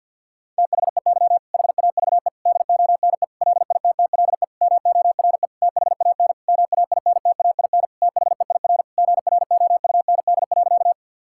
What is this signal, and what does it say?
Morse code, heard as a keyed tone, about 52 wpm, 700 Hz. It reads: THEY HAVE DONE LITTLE MORE THAN MAINTAIN THEIR GROUND.
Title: Sentences from Top 1000 Words - Review Part 5 - 50wpm